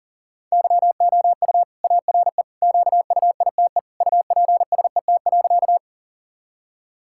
Morse code, heard as a keyed tone, about 40 wpm, 700 Hz. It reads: YOU ARE QUITE UPSET.
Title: Sentences from Top 1000 Words - Review Part 5 - 40wpm